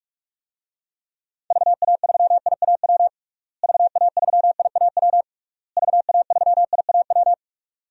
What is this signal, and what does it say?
Morse code, heard as a keyed tone, about 46 wpm, 700 Hz. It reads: VA3IAW VA3IAW VA3IAW
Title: DXCC-100 Call Signs - 6 Characters - Repeated 3x - 45wpm